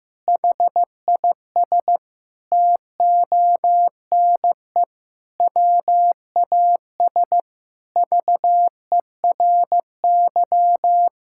Morse code, heard as a keyed tone, about 15 wpm, 700 Hz. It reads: HIS TONE WAS VERY
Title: Sentences from Top 1600 Words 15wpm